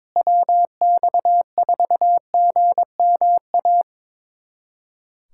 Morse code, heard as a keyed tone, about 22 wpm, 700 Hz. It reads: WX4GMA
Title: USA Callsign DE Callsign - 6 Characters - 22wpm